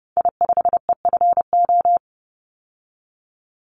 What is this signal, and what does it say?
Morse code, heard as a keyed tone, about 30 wpm, 700 Hz. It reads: I5EFO
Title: Call Signs Repeated 3x - 30wpm